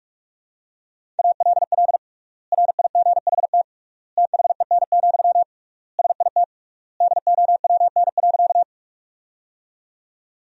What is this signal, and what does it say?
Morse code, heard as a keyed tone, about 46 wpm, 700 Hz. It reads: ALL RIGHT THEN, SIT DOWN.
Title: 3x Word-Spacing - Sentences from Top 500 Words - 45wpm